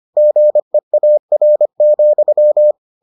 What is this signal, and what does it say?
Morse code, heard as a keyed tone, about 25 wpm, 600 Hz. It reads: GEAR,